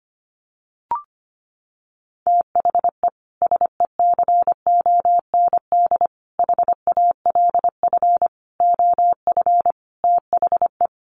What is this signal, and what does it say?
Morse code, heard as a keyed tone, about 25 wpm, 700 Hz. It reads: THE SECOND HALF OF THE
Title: Sets of 5 Words 25wpm